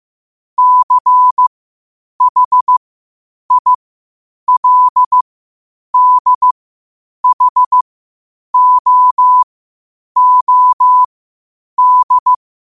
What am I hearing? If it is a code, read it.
Morse code, heard as a keyed tone, about 15 wpm, spaced as for 8 wpm, 1000 Hz. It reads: CHILDHOOD
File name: qotdfiles / 8wpm